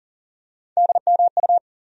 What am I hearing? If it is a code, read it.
Morse code, heard as a keyed tone, about 40 wpm, 700 Hz. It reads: DMU
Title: USA Call Sign Postfixes - 3 Letters 40wpm